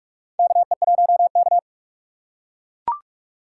Morse code, heard as a keyed tone, about 43 wpm, 700 Hz. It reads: KE1K
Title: USA Call Signs - 4 Characters - 4x Character Spacing - 45wpm